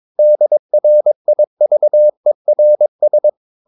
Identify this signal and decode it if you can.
Morse code, heard as a keyed tone, about 22 wpm, 600 Hz. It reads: DRIVERS
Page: Romanen Never af Ken Follet på engelsk.